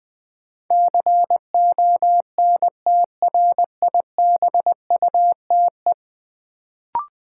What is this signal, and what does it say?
Morse code, heard as a keyed tone, about 20 wpm, 700 Hz. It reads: CONTRIBUTE
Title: Sentences from Top 1400 Words - New Word Drill Alt - 20wpm